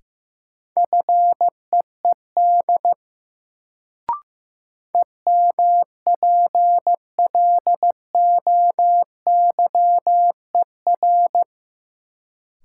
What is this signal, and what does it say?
Morse code, heard as a keyed tone, about 15 wpm, 700 Hz. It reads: FEED EMPLOYER
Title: Sentences from Top 1600 Words - New Word Drill - 15wpm